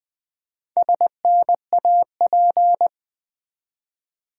Morse code, heard as a keyed tone, about 20 wpm, 700 Hz. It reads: SNAP
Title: Build A Word - 1 Letter At A Time - 4-Letter Words - 20wpm